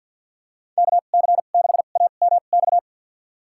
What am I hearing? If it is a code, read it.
Morse code, heard as a keyed tone, about 48 wpm, spaced as for 35 wpm, 700 Hz. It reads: KC6AMX
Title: USA Call Signs - 2x Character Spacing - 50wpm